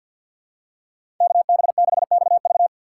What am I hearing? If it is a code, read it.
Morse code, heard as a keyed tone, about 51 wpm, 700 Hz. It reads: KB6XV
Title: USA Call Signs - 5 Characters - 4x Character Spacing - 50wpm